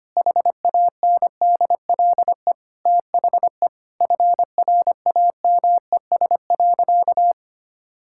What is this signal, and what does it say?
Morse code, heard as a keyed tone, about 25 wpm, 700 Hz. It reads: HANDLE THE FRAMES.
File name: practice / 25wpm / All-Age Appropriate Jokes - 25wpm